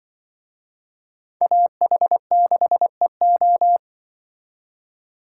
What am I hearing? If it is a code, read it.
Morse code, heard as a keyed tone, about 24 wpm, 700 Hz. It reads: AH6EO
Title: Speed Racing - USA Call Signs 20wpm